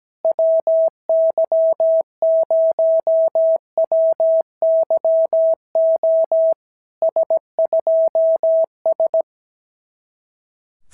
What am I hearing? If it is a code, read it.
Morse code, heard as a keyed tone, about 17 wpm, 650 Hz. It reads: WY0WYO S2S
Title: SOTA QSO Practice 17wpm